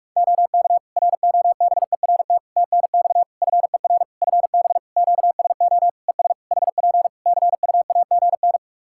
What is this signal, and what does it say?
Morse code, heard as a keyed tone, about 46 wpm, 700 Hz. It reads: OK ROBERT TNX FER FB QSO ES HP CUAGN
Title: K1USN SST QSO Practice 45wpm